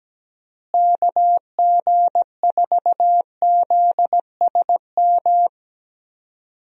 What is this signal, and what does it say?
Morse code, heard as a keyed tone, about 17 wpm, 700 Hz. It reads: KG4ZSM
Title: USA Call Signs - 6 Characters - Repeated 2x - 17wpm